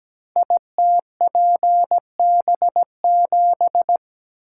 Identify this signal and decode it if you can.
Morse code, heard as a keyed tone, about 17 wpm, 700 Hz. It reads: ITPB7
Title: Code Groups - 5 Letter:Numbers - 17wpm